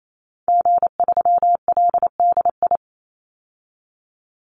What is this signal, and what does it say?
Morse code, heard as a keyed tone, about 28 wpm, 700 Hz. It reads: G3LDI
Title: Call Signs Repeated 2x - 28wpm